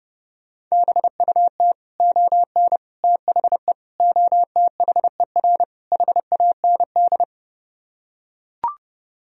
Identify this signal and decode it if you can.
Morse code, heard as a keyed tone, about 29 wpm, 700 Hz. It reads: BUT ON THE OTHER HAND